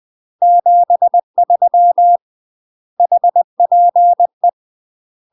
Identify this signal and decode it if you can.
Morse code, heard as a keyed tone, about 20 wpm, 700 Hz. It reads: 73 HPE
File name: UK MASTER QSOs TRAINING / 20WPM / UK MASTER QSOs 0025